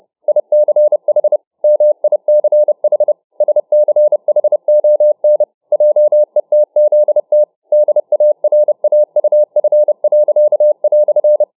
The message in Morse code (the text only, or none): ICH MICH SCHON JETZT DARAUF."